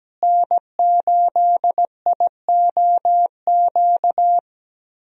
Morse code, heard as a keyed tone, about 17 wpm, 700 Hz. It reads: N8IOQ
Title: USA Call Signs - 5 Characters - Repeated 2x - 17wpm